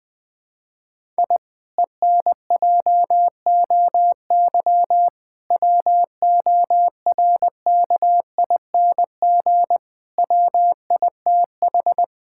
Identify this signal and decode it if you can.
Morse code, heard as a keyed tone, about 20 wpm, 700 Hz. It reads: I ENJOY WORKING WITH
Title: Sentences from Top 1000 Words - Review Part 5 - 20wpm